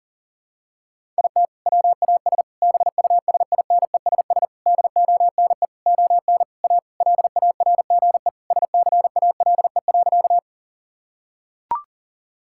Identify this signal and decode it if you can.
Morse code, heard as a keyed tone, about 40 wpm, 700 Hz. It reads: IT WAS BUSINESS DONE ON A LARGE SCALE.
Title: Sentences from Top 1000 Words - Review Part 5 - 40wpm